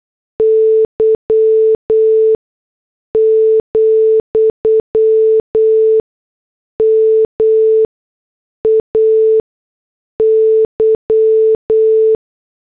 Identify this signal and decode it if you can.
Morse code, heard as a keyed tone, about 8 wpm, spaced as for 6 wpm, 440 Hz. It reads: Y, MAY
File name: morse / headlines